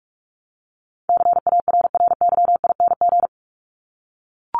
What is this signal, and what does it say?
Morse code, heard as a keyed tone, about 44 wpm, 700 Hz. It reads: CARRYING
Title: ING Suffix - Top 100 Words with Base Words 45wpm